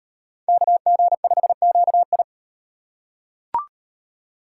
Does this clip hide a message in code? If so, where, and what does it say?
Yes: Morse code from 0.5 to 2.2 s, heard as a keyed tone, about 37 wpm, 700 Hz. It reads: KG5QI